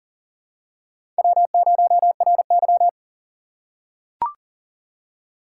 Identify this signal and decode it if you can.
Morse code, heard as a keyed tone, about 40 wpm, 700 Hz. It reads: W0RY
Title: USA Call Signs - 4 Characters - 4x Character Spacing - 40wpm